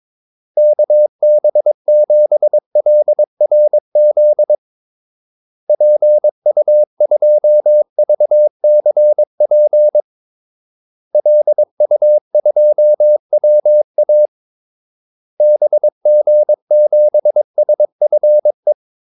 Morse code, heard as a keyed tone, about 22 wpm, 600 Hz. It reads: KB7LRZ PU2VCP LU2WA BG7SFE
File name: Callsigns-22wpm